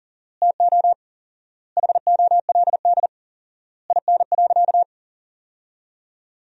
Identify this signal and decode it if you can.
Morse code, heard as a keyed tone, about 40 wpm, 700 Hz. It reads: TO HOLD IN.
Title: 4x Word-Spacing - Sentences from Top 400 Words - 40wpm